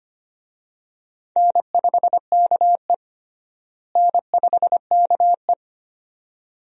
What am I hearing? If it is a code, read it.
Morse code, heard as a keyed tone, about 25 wpm, 700 Hz. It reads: N5KE N5KE
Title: USA Call Signs - 4 Characters - Repeated 2x - 25wpm